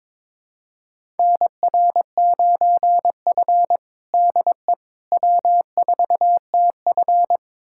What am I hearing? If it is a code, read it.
Morse code, heard as a keyed tone, about 22 wpm, 700 Hz. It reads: NR9F DE W4TF
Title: USA Callsign DE Callsign - 4 Characters - 22wpm